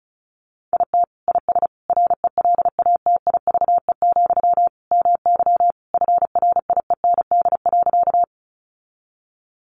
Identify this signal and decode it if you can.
Morse code, heard as a keyed tone, about 34 wpm, 700 Hz. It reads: IT IS RELATIVE, MY FRIEND.
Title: Sentences from Top 1400 Words 35wpm